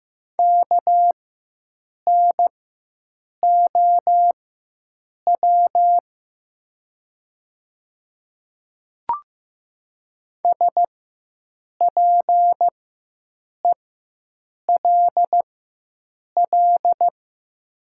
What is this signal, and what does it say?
Morse code, heard as a keyed tone, about 15 wpm, spaced as for 7 wpm, 700 Hz. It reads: KNOW SPELL
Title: 4x Character Spacing - Top 300 Words - 15wpm